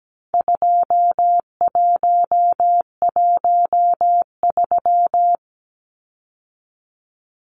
Morse code, heard as a keyed tone, about 17 wpm, 700 Hz. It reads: 2113
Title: Numbers - 4-Digit 17wpm